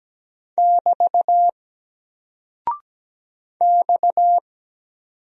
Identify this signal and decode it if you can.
Morse code, heard as a keyed tone, about 17 wpm, 700 Hz. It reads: <BT> X
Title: Single Character (Letter:Number:Punctuation:Prosign) 17wpm